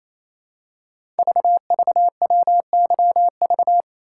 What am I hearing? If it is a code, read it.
Morse code, heard as a keyed tone, about 27 wpm, 700 Hz. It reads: VVWYV
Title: Code Groups - 5 Letters - 28wpm